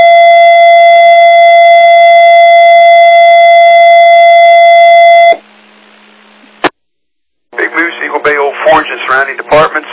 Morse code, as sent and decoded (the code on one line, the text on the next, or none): none